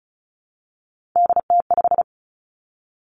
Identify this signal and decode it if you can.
Morse code, heard as a keyed tone, about 36 wpm, 700 Hz. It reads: DT5
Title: Three Challenging Letter-Number Combinations 35wpm